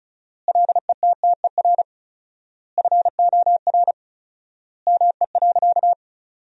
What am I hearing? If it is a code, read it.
Morse code, heard as a keyed tone, about 34 wpm, 700 Hz. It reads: LETTER FOR ME.
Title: 4x Word-Spacing - Sentences from Top 300 Words - 35wpm